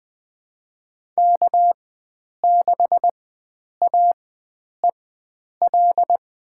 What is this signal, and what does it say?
Morse code, heard as a keyed tone, about 20 wpm, spaced as for 9 wpm, 700 Hz. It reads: K6AEL
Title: USA Call Signs - 4x Character Spacing - 20wpm